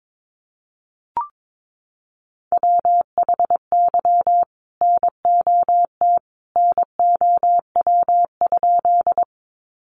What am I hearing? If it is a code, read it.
Morse code, heard as a keyed tone, about 22 wpm, 700 Hz. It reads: WHY NOT NOW?
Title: Sentences from Top 600 Words 22wpm